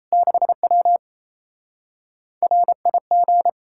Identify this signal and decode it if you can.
Morse code, heard as a keyed tone, about 37 wpm, 700 Hz. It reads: 6W RIG